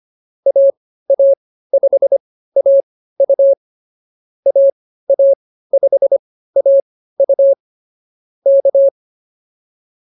Morse code, heard as a keyed tone, about 24 wpm, spaced as for 15 wpm, 560 Hz. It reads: AA5AU AA5AU K